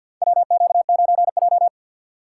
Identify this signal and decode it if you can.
Morse code, heard as a keyed tone, about 53 wpm, 700 Hz. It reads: WQ9J
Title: USA Call Signs - 4 Characters - 2x Character Spacing - 50wpm